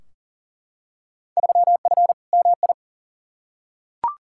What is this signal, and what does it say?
Morse code, heard as a keyed tone, about 40 wpm, 700 Hz. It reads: 3F MI
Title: Field Day Exchanges - 40wpm